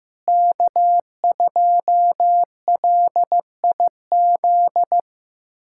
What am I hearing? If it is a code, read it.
Morse code, heard as a keyed tone, about 16 wpm, 700 Hz. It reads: K2LIZ
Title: USA Callsign DE Callsign 15wpm